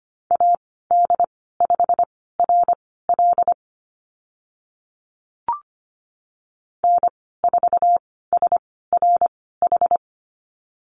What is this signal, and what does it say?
Morse code, heard as a keyed tone, about 25 wpm, spaced as for 16 wpm, 700 Hz. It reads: AD5RL N4SRH
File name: Lesson 15g - Number 5 - Call Signs 25wpm